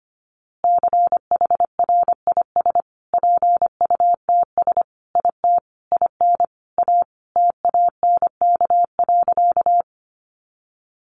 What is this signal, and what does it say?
Morse code, heard as a keyed tone, about 25 wpm, 700 Hz. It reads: CHRIS PUTS IT IN A TANK.
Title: Easy Phrases 25wpm